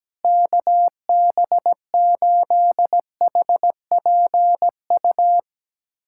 Morse code, heard as a keyed tone, about 17 wpm, 700 Hz. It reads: KB8HPU
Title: USA Callsign DE Callsign - 6 Characters - 17wpm